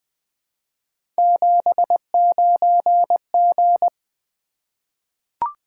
Morse code, heard as a keyed tone, about 20 wpm, 700 Hz. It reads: 79G
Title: Three Letter-Number Combinations - 20wpm